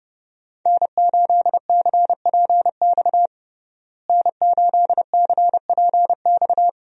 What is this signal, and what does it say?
Morse code, heard as a keyed tone, about 31 wpm, 700 Hz. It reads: N8CPX N8CPX
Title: USA Call Signs - 5 Characters - Repeated 3x - 30wpm